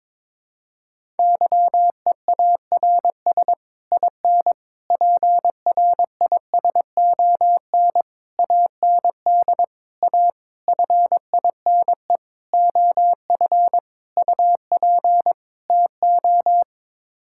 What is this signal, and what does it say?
Morse code, heard as a keyed tone, about 22 wpm, 700 Hz. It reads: YEARS IN PRISON AND A FINE OF UP TO
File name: Sets of 9 Words - 22wpm